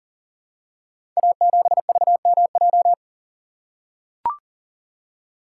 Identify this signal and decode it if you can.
Morse code, heard as a keyed tone, about 43 wpm, 690 Hz. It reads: A7VMJ
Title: Code Groups - 5 Letter:Numbers - 40wpm